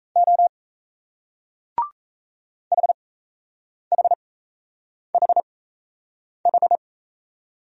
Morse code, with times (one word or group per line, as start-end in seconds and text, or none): 0.2-0.5 O
2.7-6.8 HHHH